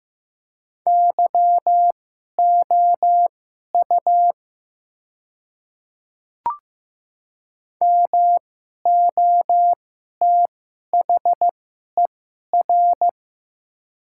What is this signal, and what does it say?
Morse code, heard as a keyed tone, about 15 wpm, spaced as for 11 wpm, 700 Hz. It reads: YOU MOTHER
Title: 2x Character Spacing - Top 300 Words - 15wpm